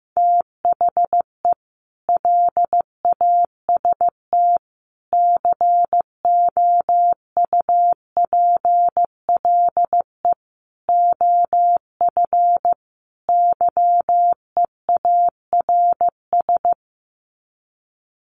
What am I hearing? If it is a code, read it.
Morse code, heard as a keyed tone, about 15 wpm, 700 Hz. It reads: THE LAST COUPLE OF YEARS
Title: Sets of 5 Words 15wpm